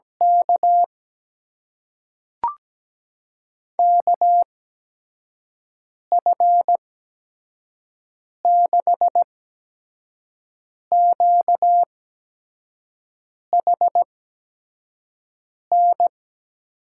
Morse code, heard as a keyed tone, about 17 wpm, spaced as for 5 wpm, 700 Hz. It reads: K KF6QHN
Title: USA Call Signs - 8x Character Spacing - 17wpm